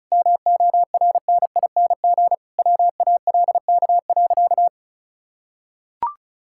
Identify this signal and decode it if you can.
Morse code, heard as a keyed tone, about 34 wpm, 700 Hz. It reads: MORNING WALK.
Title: Sentences from Top 1000 Words - Review Part 5 - 35wpm